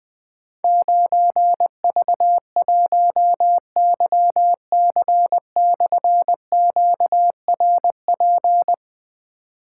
Morse code, heard as a keyed tone, about 20 wpm, 700 Hz. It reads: 9V1YC/QRP
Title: Difficult Call Signs 20wpm